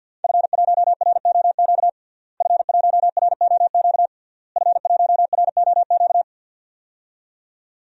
Morse code, heard as a keyed tone, about 50 wpm, 700 Hz. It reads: F1ROQ F1ROQ F1ROQ
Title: DXCC-100 Call Signs - 5 Characters - Repeated 3x - 50wpm